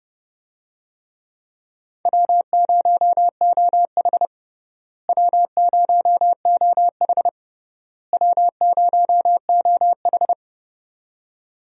Morse code, heard as a keyed tone, about 30 wpm, 700 Hz. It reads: W0OH W0OH W0OH
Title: USA Call Signs - 4 Characters - Repeated 3x - 30wpm